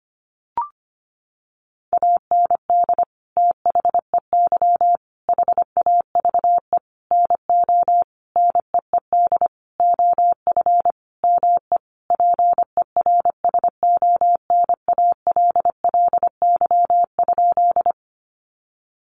Morse code, heard as a keyed tone, about 25 wpm, 700 Hz. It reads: AND THEY HAVE NO NEED OF ME PERSONALLY?
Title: Sentences from Top 1400 Words 25wpm